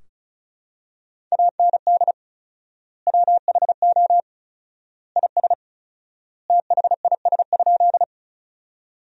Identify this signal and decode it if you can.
Morse code, heard as a keyed tone, about 36 wpm, 700 Hz. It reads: AND WHO IS THIS?
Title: 4x Word-Spacing - Sentences from Top 100 Words - 35wpm